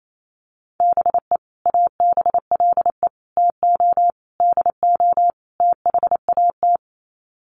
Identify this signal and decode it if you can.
Morse code, heard as a keyed tone, about 28 wpm, 700 Hz. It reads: BE ABLE TO DO THAT
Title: Sets of 5 Words Encore2 28wpm